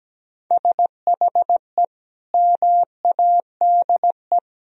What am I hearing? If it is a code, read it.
Morse code, heard as a keyed tone, about 17 wpm, 700 Hz. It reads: SHE MADE
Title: Sentences from Top 1000 Words - Review Part 1 - 17wpm